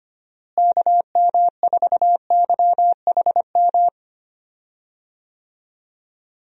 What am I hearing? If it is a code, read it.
Morse code, heard as a keyed tone, about 25 wpm, 700 Hz. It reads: KM4YHM
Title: USA Call Signs - 6 Characters - Repeated 3x - 25wpm